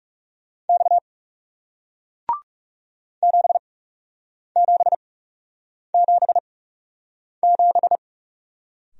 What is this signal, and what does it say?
Morse code, heard as a keyed tone, about 39 wpm, spaced as for 8 wpm, 700 Hz. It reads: X 7777